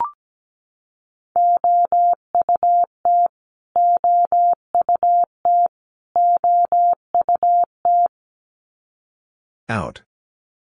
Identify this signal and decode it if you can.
Morse code, heard as a keyed tone, about 17 wpm, 700 Hz. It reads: OUT OUT OUT
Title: Top 500 Words Repeated 17wpm